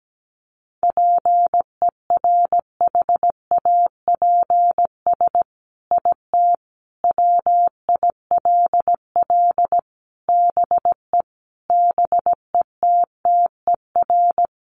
Morse code, heard as a keyed tone, about 17 wpm, 700 Hz. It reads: PERHAPS IT WILL BE BETTER
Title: Sentences from Top 1500 Words 17wpm